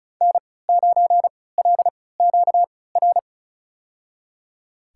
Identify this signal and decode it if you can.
Morse code, heard as a keyed tone, about 34 wpm, spaced as for 20 wpm, 700 Hz. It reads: N9LQR
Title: USA Call Signs - 3x Character Spacing - 35wpm